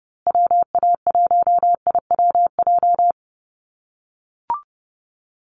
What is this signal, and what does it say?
Morse code, heard as a keyed tone, about 30 wpm, 700 Hz. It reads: WA1IWJ
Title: USA Call Signs Repeated 2x 30wpm